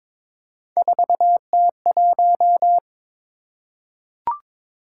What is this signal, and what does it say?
Morse code, heard as a keyed tone, about 22 wpm, 700 Hz. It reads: 4T1
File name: practice / 22wpm / Three Letter-Number Combinations - 22wpm